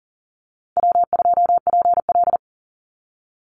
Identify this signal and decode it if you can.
Morse code, heard as a keyed tone, about 40 wpm, 700 Hz. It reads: W2PL
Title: Easy Call Signs - 40wpm